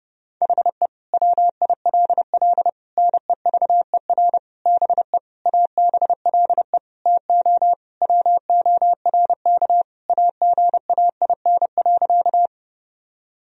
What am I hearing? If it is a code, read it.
Morse code, heard as a keyed tone, about 30 wpm, 700 Hz. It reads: HE WILL NEVER BE ABLE TO WORK AGAIN.
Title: Sentences from Top 1000 Words - Review Part 2 - 30wpm